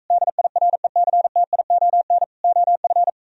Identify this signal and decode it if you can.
Morse code, heard as a keyed tone, about 45 wpm, 700 Hz. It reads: DIRECTION OF